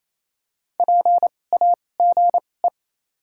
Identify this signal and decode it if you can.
Morse code, heard as a keyed tone, about 28 wpm, spaced as for 20 wpm, 700 Hz. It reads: PAGE
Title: 2x Character Spacing - Top 300 Words - 28wpm